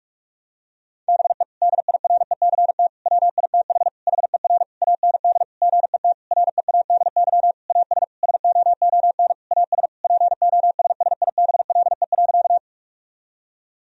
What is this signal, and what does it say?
Morse code, heard as a keyed tone, about 46 wpm, 700 Hz. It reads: BE DIRECT WITH HER AND GET READY AS SOON AS POSSIBLE.
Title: Sentences from Top 1000 Words - Review Part 2 - 45wpm